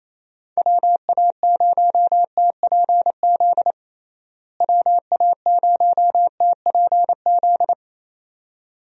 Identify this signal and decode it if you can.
Morse code, heard as a keyed tone, about 28 wpm, 700 Hz. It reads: WA0TPZ WA0TPZ
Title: USA Call Signs - 6 Characters - Repeated 2x - 28wpm